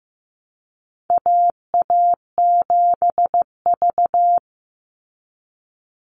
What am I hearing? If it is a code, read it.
Morse code, heard as a keyed tone, about 15 wpm, 700 Hz. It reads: AA7V
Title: Call Signs Repeated 2x - 15wpm